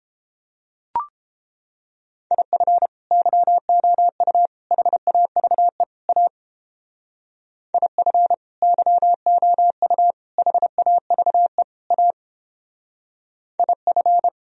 Morse code, heard as a keyed tone, about 29 wpm, 700 Hz. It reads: IF YOU HAVE A IF YOU HAVE A IF